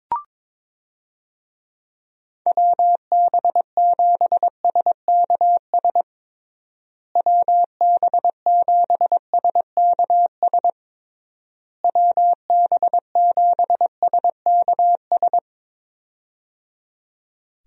Morse code, heard as a keyed tone, about 22 wpm, 700 Hz. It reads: WB7SKS WB7SKS WB7SKS
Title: USA Call Signs - 6 Characters - Repeated 3x - 22wpm